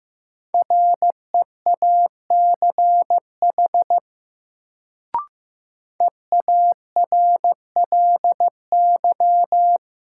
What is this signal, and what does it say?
Morse code, heard as a keyed tone, about 15 wpm, 700 Hz. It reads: REACH EARLY
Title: Sentences from Top 400 Words - New Word Drill - 15wpm